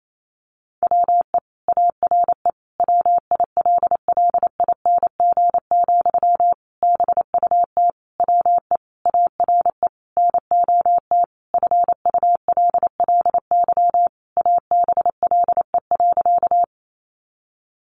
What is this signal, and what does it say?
Morse code, heard as a keyed tone, about 28 wpm, 700 Hz. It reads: WE ARE WILLING, BUT WE ARE NOT FULLY ABLE.
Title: Sentences from Top 1300 Words 28wpm